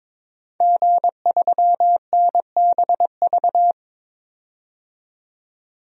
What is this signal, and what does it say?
Morse code, heard as a keyed tone, about 22 wpm, 700 Hz. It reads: G3NBV
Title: DXCC-100 Call Signs - 5 Characters - 22wpm